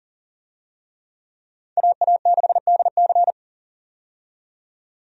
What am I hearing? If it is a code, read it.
Morse code, heard as a keyed tone, about 40 wpm, 700 Hz. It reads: AA6DC
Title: USA Call Signs - 5 Characters - 40wpm